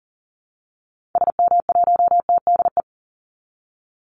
Morse code, heard as a keyed tone, about 40 wpm, 700 Hz. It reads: SM1TDE
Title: Call Signs Repeated 3x - 40wpm